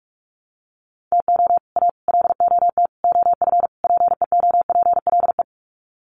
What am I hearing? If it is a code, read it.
Morse code, heard as a keyed tone, about 44 wpm, 700 Hz. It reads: TO A LOT OF PEOPLE
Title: Sets of 5 Words Encore 45wpm